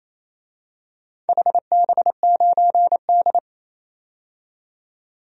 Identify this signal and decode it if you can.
Morse code, heard as a keyed tone, about 28 wpm, 700 Hz. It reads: HB9D
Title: DXCC-100 Call Signs - 4 Characters - 28wpm